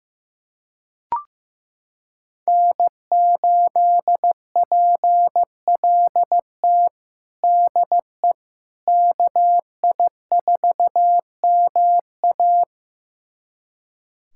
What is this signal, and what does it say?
Morse code, heard as a keyed tone, about 15 wpm, 700 Hz. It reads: N8PLT DE KI4MA
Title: USA Callsign DE Callsign - 5 Characters - 15wpm